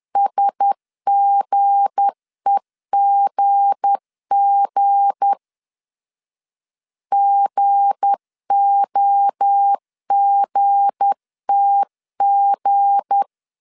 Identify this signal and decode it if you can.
Morse code, heard as a keyed tone, about 10 wpm, 800 Hz. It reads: SGEGG GOGTG
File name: LEZIONI / 7 GF9Z / 7 G 10wpm 2501